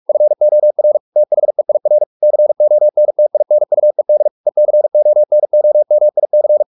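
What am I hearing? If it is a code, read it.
Morse code, heard as a keyed tone, about 44 wpm, 600 Hz. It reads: FOR THEIR CONTINUED ECONOMIC
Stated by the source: Morse practice at 45 wpm, the novel Never by Ken Follett in English, from CW med Gnister.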